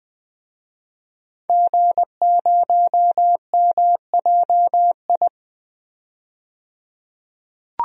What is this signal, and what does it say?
Morse code, heard as a keyed tone, about 20 wpm, 700 Hz. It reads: G0MJI
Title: DXCC-100 Call Signs - 5 Characters - 20wpm